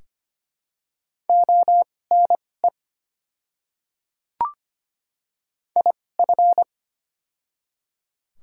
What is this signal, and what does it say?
Morse code, heard as a keyed tone, about 25 wpm, spaced as for 18 wpm, 700 Hz. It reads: ONE IF
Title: 2x Character Spacing - Top 100 Words - 25wpm